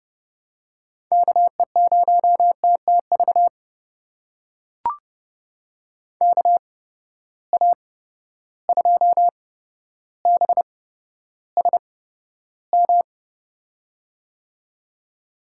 Morse code, heard as a keyed tone, about 30 wpm, spaced as for 13 wpm, 700 Hz. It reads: KE0TTV KA2BSM
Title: USA Call Signs - 6 Characters - 8x Character Spacing - 30wpm